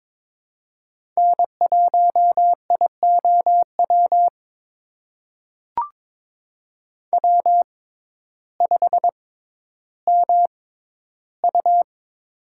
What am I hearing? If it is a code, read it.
Morse code, heard as a keyed tone, about 22 wpm, spaced as for 12 wpm, 700 Hz. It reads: N1IOW W5MU
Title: USA Call Signs - 6x Character Spacing - 22wpm